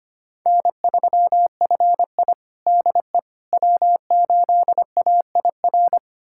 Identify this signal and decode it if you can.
Morse code, heard as a keyed tone, about 25 wpm, 700 Hz. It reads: N3FI DE W8AIR
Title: USA Callsign DE Callsign 25wpm